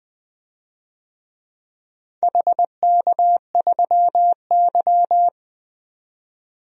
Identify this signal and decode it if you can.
Morse code, heard as a keyed tone, about 20 wpm, 700 Hz. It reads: HK3Y
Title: DXCC-100 Call Signs - 4 Characters - 20wpm